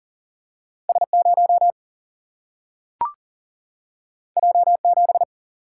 Morse code, heard as a keyed tone, about 40 wpm, 700 Hz. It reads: S 0 J 7
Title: Single-Letter Single-Number 40wpm